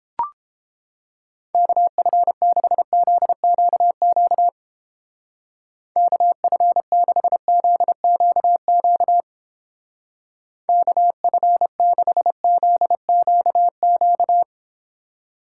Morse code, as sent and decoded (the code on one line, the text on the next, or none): -.- ..-. -.... --.. --.- --.- / -.- ..-. -.... --.. --.- --.- / -.- ..-. -.... --.. --.- --.-
KF6ZQQ KF6ZQQ KF6ZQQ